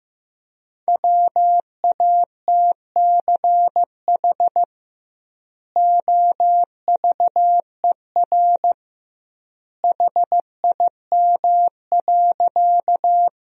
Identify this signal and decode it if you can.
Morse code, heard as a keyed tone, about 15 wpm, 700 Hz. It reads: WATCH OVER HIM.
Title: 2x Word-Spacing - Sentences from Top 300 Words - 15wpm